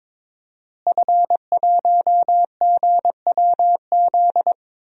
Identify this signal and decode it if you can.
Morse code, heard as a keyed tone, about 22 wpm, 700 Hz. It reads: F1GWZ
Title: DXCC-100 Call Signs - 5 Characters - Repeated 2x - 22wpm